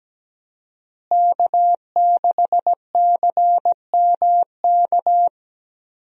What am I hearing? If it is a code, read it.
Morse code, heard as a keyed tone, about 17 wpm, 700 Hz. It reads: K6CMK
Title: USA Call Signs - 5 Characters - Repeated 3x - 17wpm